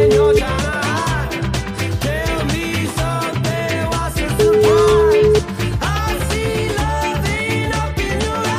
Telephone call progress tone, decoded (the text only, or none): ringback tone